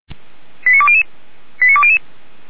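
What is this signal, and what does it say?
Five-tone selective call (ZVEI), heard as quick runs of five tones, tones of about 70 ms. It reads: 89200 89200
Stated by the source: ZVEI five-tone selective call received by SDR, address 89131.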